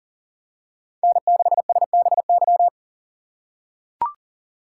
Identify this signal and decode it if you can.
Morse code, heard as a keyed tone, about 40 wpm, 700 Hz. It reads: N6SBY
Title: USA Call Signs - 2x Character Spacing - 40wpm